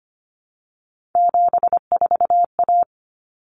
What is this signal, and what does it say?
Morse code, heard as a keyed tone, about 25 wpm, 700 Hz. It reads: 74A
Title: Common Cut Numbers - 3-Digit 25wpm